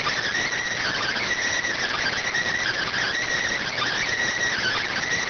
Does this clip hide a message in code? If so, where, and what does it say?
Yes: Emergency Alert System (SAME) tones (header bursts) from 0.8 to 5.0 s, heard as harsh buzzing bursts.